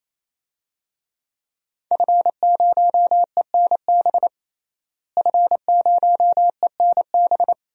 Morse code, heard as a keyed tone, about 28 wpm, 700 Hz. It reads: F0ENB F0ENB
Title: DXCC-100 Call Signs - 5 Characters - Repeated 3x - 28wpm